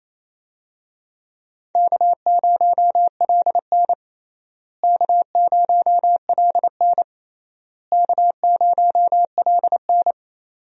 Morse code, heard as a keyed tone, about 28 wpm, 700 Hz. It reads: K0LN K0LN K0LN
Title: USA Call Signs - 4 Characters - Repeated 3x - 28wpm